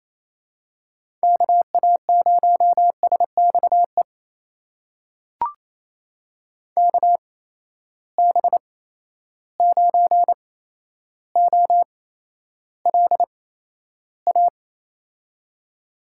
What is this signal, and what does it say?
Morse code, heard as a keyed tone, about 27 wpm, spaced as for 12 wpm, 700 Hz. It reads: KA0SXE KB9OLA
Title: USA Call Signs - 6 Characters - 8x Character Spacing - 28wpm